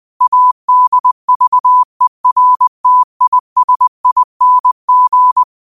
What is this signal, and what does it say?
Morse code, heard as a keyed tone, about 20 wpm, 1000 Hz. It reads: ADVERTISING